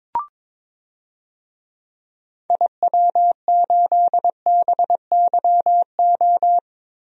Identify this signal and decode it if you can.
Morse code, heard as a keyed tone, about 22 wpm, 700 Hz. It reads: IW8BYO
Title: DXCC-100 Call Signs - 6 Characters - 22wpm